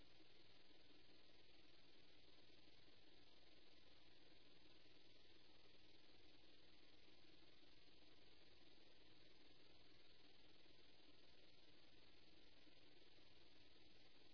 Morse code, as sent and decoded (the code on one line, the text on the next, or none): none